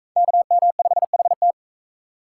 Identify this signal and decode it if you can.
Morse code, heard as a keyed tone, about 37 wpm, 700 Hz. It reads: KM5HT